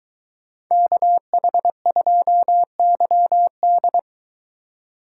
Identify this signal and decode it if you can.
Morse code, heard as a keyed tone, about 22 wpm, 700 Hz. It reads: KH2YD